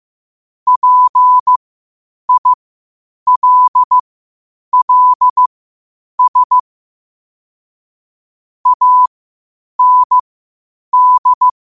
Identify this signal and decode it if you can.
Morse code, heard as a keyed tone, about 15 wpm, spaced as for 8 wpm, 1000 Hz. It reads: PILLS AND
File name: qotdfiles / 8wpm